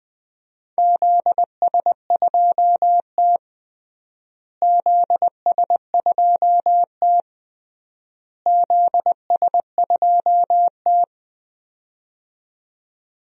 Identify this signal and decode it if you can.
Morse code, heard as a keyed tone, about 21 wpm, 700 Hz. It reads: ZS2T ZS2T ZS2T
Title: DXCC-100 Call Signs - 4 Characters - Repeated 3x - 20wpm